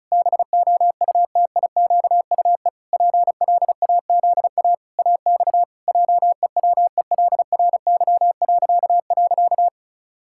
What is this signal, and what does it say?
Morse code, heard as a keyed tone, about 34 wpm, 700 Hz. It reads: BOUTIQUE PLAZA AX JEWELRY..
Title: Pangram Challenge v2 - 35wpm